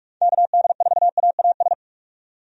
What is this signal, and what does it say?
Morse code, heard as a keyed tone, about 44 wpm, 700 Hz. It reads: KD4AAS
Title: USA Call Signs - 6 Characters - 45wpm